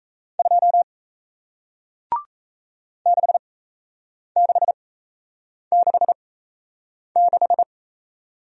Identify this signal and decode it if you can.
Morse code, heard as a keyed tone, about 38 wpm, spaced as for 8 wpm, 700 Hz. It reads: 2 6666